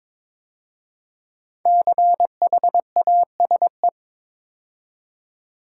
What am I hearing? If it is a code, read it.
Morse code, heard as a keyed tone, about 22 wpm, 700 Hz. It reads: CHASE
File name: 5 Letter Words - First 3 Letters Given - 22wpm